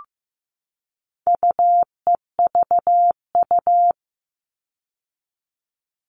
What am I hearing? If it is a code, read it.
Morse code, heard as a keyed tone, about 15 wpm, 700 Hz. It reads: UEVU
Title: All Cut Numbers - 4-Digit 15wpm